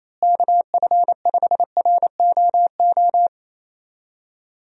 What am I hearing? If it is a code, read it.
Morse code, heard as a keyed tone, about 28 wpm, 700 Hz. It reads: KF5ROO
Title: USA Call Signs - 6 Characters - Repeated 3x - 28wpm